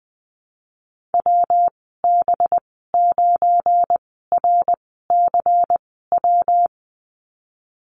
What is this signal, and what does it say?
Morse code, heard as a keyed tone, about 20 wpm, spaced as for 15 wpm, 700 Hz. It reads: WB9RCW
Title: 186 Lesson 30g - Letter b - Call Signs 20wpm